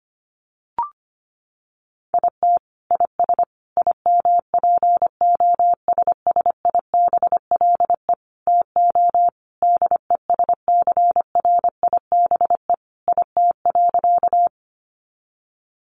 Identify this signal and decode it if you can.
Morse code, heard as a keyed tone, about 25 wpm, 700 Hz. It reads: IT IS IMPOSSIBLE TO DESCRIBE IT.
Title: Sentences from Top 1100 Words 25wpm